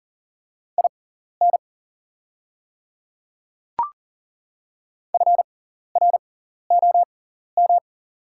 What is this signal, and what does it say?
Morse code, heard as a keyed tone, about 40 wpm, spaced as for 14 wpm, 700 Hz. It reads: IN FROM
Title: 6x Character Spacing - Top 100 Words - 40wpm